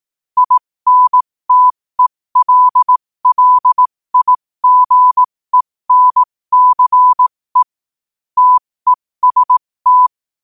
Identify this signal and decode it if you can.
Morse code, heard as a keyed tone, about 18 wpm, spaced as for 15 wpm, 1000 Hz. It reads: INTELLIGENCE TEST
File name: qotdfiles / 15wpm